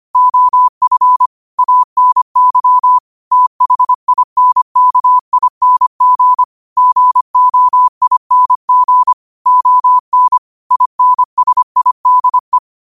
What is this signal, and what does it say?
Morse code, heard as a keyed tone, about 25 wpm, 1000 Hz. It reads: OF ANY THINKING GOING ON INSIDE
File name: qotdfiles / 25wpm